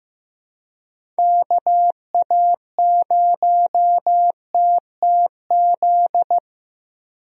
Morse code, heard as a keyed tone, about 15 wpm, 700 Hz. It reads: KA0TTZ
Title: USA Call Signs - 6 Characters - Repeated 2x - 15wpm